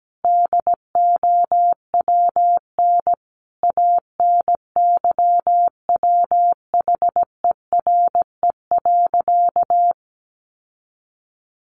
Morse code, heard as a keyed tone, about 17 wpm, 700 Hz. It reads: DOWN ANYWHERE.
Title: Sentences from Top 1400 Words 17wpm